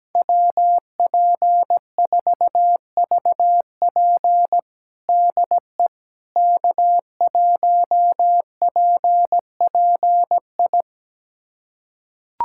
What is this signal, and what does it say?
Morse code, heard as a keyed tone, about 17 wpm, 700 Hz. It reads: WP4VP DE K1PPI
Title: USA Callsign DE Callsign - 5 Characters - 17wpm